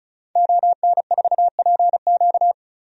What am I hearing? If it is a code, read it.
Morse code, heard as a keyed tone, about 36 wpm, 700 Hz. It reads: ON4PQ
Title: DXCC-100 Call Signs - 5 Characters - 35wpm